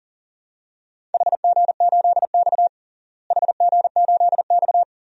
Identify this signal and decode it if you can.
Morse code, heard as a keyed tone, about 39 wpm, 700 Hz. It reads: HG8X HG8X
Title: DXCC-100 Call Signs - 4 Characters - Repeated 2x - 40wpm